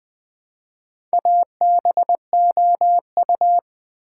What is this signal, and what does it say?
Morse code, heard as a keyed tone, about 20 wpm, 700 Hz. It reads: ABOU
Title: Build A Word - 1 Letter At A Time - 5-Letter Words - 20wpm